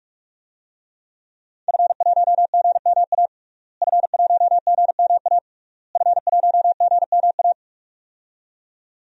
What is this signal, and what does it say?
Morse code, heard as a keyed tone, about 41 wpm, 700 Hz. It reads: F1GMA F1GMA F1GMA
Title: DXCC-100 Call Signs - 5 Characters - Repeated 3x - 45wpm